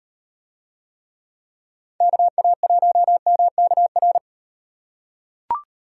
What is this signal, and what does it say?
Morse code, heard as a keyed tone, about 36 wpm, 700 Hz. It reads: KA1MKR